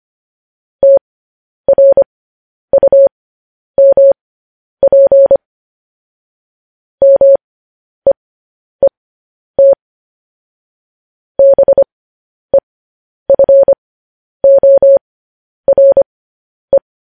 Morse code, heard as a keyed tone, about 25 wpm, spaced as for 10 wpm, 550 Hz. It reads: TRUMP MEET BEFORE
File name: podcast / News headlines-1x-25x10